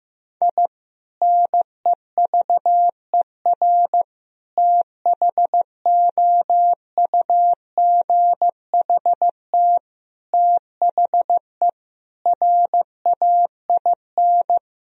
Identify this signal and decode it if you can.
Morse code, heard as a keyed tone, about 16 wpm, 700 Hz. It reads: I NEVER THOUGHT THE RAIN
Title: Sentences from Top 1000 Words - Review Part 1 - 15wpm